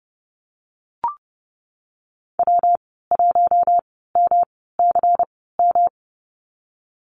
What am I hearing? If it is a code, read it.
Morse code, heard as a keyed tone, about 30 wpm, spaced as for 17 wpm, 700 Hz. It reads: W1MCM
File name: Lesson 19g - Letter w - Call Signs 30wpm